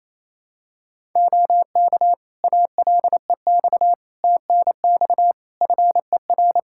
Morse code, heard as a keyed tone, about 27 wpm, 700 Hz. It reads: OK ALEX TNX FER
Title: Practice QSO - Exchange 4 - Ending - 28wpm